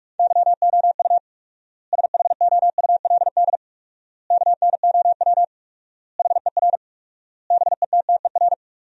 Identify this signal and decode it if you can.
Morse code, heard as a keyed tone, about 46 wpm, 700 Hz. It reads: YOU SHOULD KNOW HER BETTER
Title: 4x Word-Spacing - Sentences from Top 400 Words - 45wpm